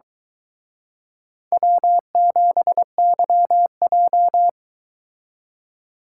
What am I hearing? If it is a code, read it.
Morse code, heard as a keyed tone, about 23 wpm, 700 Hz. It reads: W7YJ